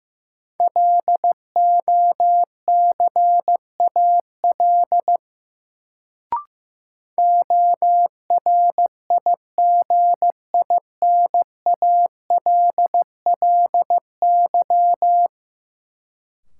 Morse code, heard as a keyed tone, about 15 wpm, 700 Hz. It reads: LOCAL ORIGINALLY
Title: Sentences from Top 1500 Words - New Word Drill Alt - 15wpm